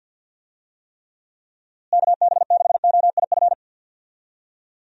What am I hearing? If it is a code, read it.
Morse code, heard as a keyed tone, about 53 wpm, 700 Hz. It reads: KB6OIF
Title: USA Call Signs - 6 Characters - Repeated 2x - 50wpm